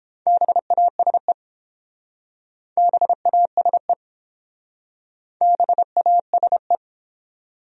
Morse code, heard as a keyed tone, about 28 wpm, 700 Hz. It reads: BASE BASE BASE